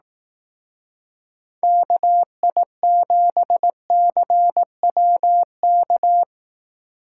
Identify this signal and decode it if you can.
Morse code, heard as a keyed tone, about 18 wpm, 700 Hz. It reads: KI7CWK
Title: Speed Racing - USA 2x3 Call Signs 15wpm